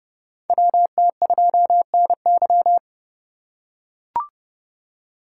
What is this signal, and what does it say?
Morse code, heard as a keyed tone, about 30 wpm, 700 Hz. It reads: WT2NY
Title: USA Call Signs - 5 Characters - 2x Character Spacing - 30wpm